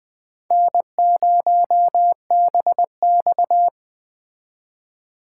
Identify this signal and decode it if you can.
Morse code, heard as a keyed tone, about 21 wpm, 700 Hz. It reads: N0BX